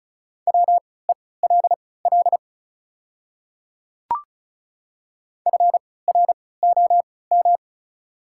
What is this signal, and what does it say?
Morse code, heard as a keyed tone, about 36 wpm, spaced as for 20 wpm, 700 Hz. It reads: WELL FROM
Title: 3x Character Spacing - Top 300 Words - 35wpm